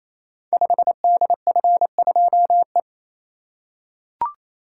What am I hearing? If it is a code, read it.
Morse code, heard as a keyed tone, about 28 wpm, 700 Hz. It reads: 5DF2E
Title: Code Groups - 5 Letter:Numbers - 28wpm